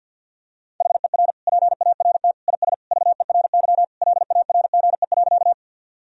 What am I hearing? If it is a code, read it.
Morse code, heard as a keyed tone, about 52 wpm, 700 Hz. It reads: HER PART IS VERY LARGE.
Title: Sentences from Top 1000 Words - Review Part 1 - 50wpm